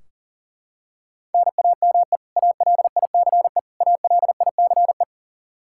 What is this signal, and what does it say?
Morse code, heard as a keyed tone, about 37 wpm, 700 Hz. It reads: NAME ALICE ALICE
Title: Practice QSO - Exchange 1 - Beginning - 40wpm